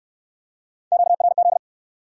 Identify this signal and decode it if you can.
Morse code, heard as a keyed tone, about 67 wpm, 700 Hz. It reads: QRZ